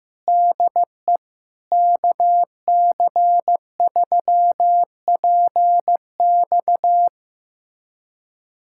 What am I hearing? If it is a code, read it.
Morse code, heard as a keyed tone, about 15 wpm, 700 Hz. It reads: DE KC3PX
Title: USA Callsign DE Callsign - 5 Characters - 15wpm